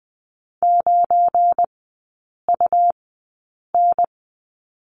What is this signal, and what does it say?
Morse code, heard as a keyed tone, about 20 wpm, spaced as for 8 wpm, 700 Hz. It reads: 9UN
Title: 172 Lesson 28e - Number 9 - Three Characters 20wpm